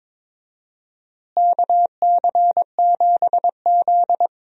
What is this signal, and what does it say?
Morse code, heard as a keyed tone, about 22 wpm, 700 Hz. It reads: KC7Z
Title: USA Call Signs - 4 Characters - Repeated 2x - 22wpm